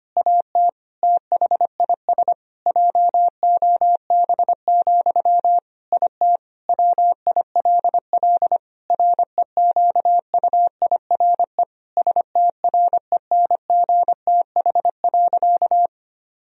Morse code, heard as a keyed tone, about 25 wpm, 700 Hz. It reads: AT THIS JOB, IT WILL REQUIRE STRENGTH.
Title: Sentences from Top 1000 Words - Review Part 4 - 25wpm